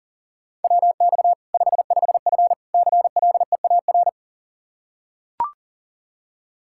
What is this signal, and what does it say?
Morse code, heard as a keyed tone, about 40 wpm, 700 Hz. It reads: WX 55F CLEAR
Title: Practice QSO - Exchange 2 - Info - 40wpm